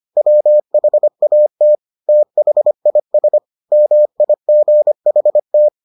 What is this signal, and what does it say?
Morse code, heard as a keyed tone, about 25 wpm, 600 Hz. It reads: WHAT THIS MIGHT